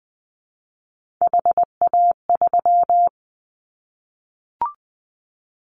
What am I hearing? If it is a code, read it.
Morse code, heard as a keyed tone, about 20 wpm, 700 Hz. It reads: HA3
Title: Callsign Prefixes for Top 25 DXCC - Repeated 3x - 20wpm